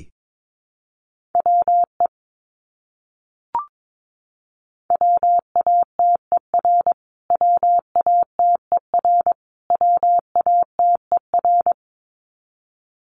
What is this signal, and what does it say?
Morse code, heard as a keyed tone, about 22 wpm, 700 Hz. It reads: WE WATER WATER WATER
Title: Top 500 Words Repeated 22wpm